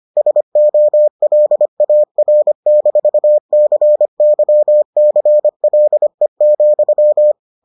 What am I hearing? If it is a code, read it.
Morse code, heard as a keyed tone, about 25 wpm, 600 Hz. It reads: SOLAR-CYCLE,